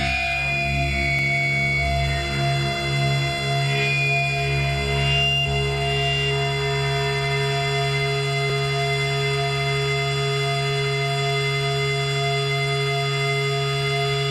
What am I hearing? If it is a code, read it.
Dial-up modem handshake.